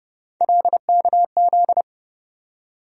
Morse code, heard as a keyed tone, about 30 wpm, 700 Hz. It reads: LKZ
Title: USA Call Sign Postfixes - 3 Letters 30wpm